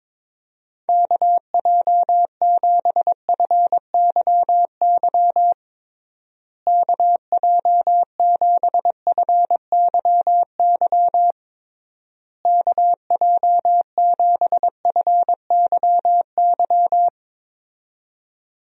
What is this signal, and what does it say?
Morse code, heard as a keyed tone, about 22 wpm, 700 Hz. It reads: KJ7FYY KJ7FYY KJ7FYY
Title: USA Call Signs - 6 Characters - Repeated 3x - 22wpm